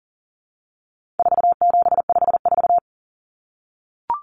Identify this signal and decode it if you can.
Morse code, heard as a keyed tone, about 37 wpm, 700 Hz. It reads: 4754
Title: Common Cut Numbers - 4-Digit 40wpm